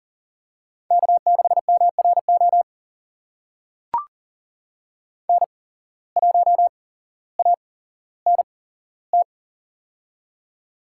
Morse code, heard as a keyed tone, about 38 wpm, spaced as for 17 wpm, 700 Hz. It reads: K6MRO N1ANT
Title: USA Call Signs - 5 Characters - 8x Character Spacing - 40wpm